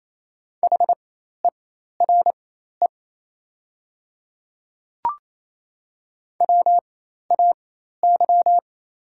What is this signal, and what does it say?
Morse code, heard as a keyed tone, about 28 wpm, spaced as for 13 wpm, 700 Hz. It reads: HERE WAY
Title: 4x Character Spacing - Top 300 Words - 28wpm